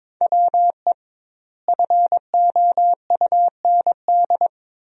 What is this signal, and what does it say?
Morse code, heard as a keyed tone, about 21 wpm, 700 Hz. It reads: WE FOUND
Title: 2x Word-Spacing - Sentences from Top 300 Words - 22wpm